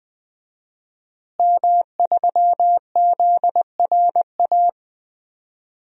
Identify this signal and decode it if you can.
Morse code, heard as a keyed tone, about 21 wpm, 700 Hz. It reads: M3ZRA
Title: DXCC-100 Call Signs - 5 Characters - Repeated 2x - 20wpm